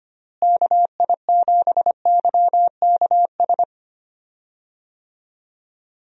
Morse code, heard as a keyed tone, about 25 wpm, 700 Hz. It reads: KI7YKS
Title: USA Call Signs - 6 Characters - Repeated 3x - 25wpm